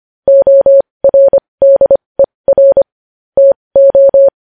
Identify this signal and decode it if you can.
Morse code, heard as a keyed tone, about 25 wpm, spaced as for 20 wpm, 550 Hz. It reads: ORDER TO
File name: podcast / News headlines-1x-25x20